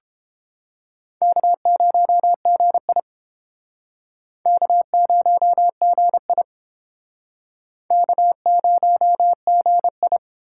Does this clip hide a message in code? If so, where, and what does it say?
Yes: Morse code from 1.2 to 10.2 s, heard as a keyed tone, about 30 wpm, 700 Hz. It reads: K0GI K0GI K0GI